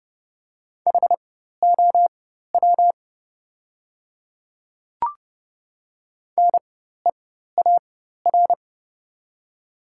Morse code, heard as a keyed tone, about 30 wpm, spaced as for 14 wpm, 700 Hz. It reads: HOW NEAR
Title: 4x Character Spacing - Top 200 Words - 30wpm